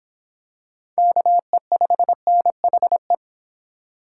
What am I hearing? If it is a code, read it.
Morse code, heard as a keyed tone, about 25 wpm, 700 Hz. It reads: KE5NHE